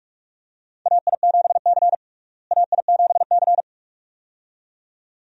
Morse code, heard as a keyed tone, about 46 wpm, 700 Hz. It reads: AI7C AI7C
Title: USA Call Signs - 4 Characters - Repeated 2x - 45wpm